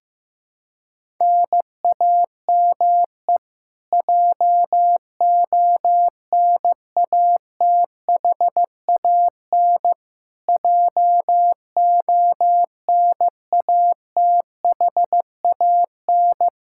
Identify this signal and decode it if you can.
Morse code, heard as a keyed tone, about 15 wpm, 700 Hz. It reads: NAME JONATHAN JONATHAN
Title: Practice QSO - Exchange 1 - Beginning - 15wpm